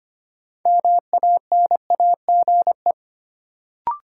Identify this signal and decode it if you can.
Morse code, heard as a keyed tone, about 25 wpm, 700 Hz. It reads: MANAGE
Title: Sentences from Top 1400 Words - New Word Drill - 25wpm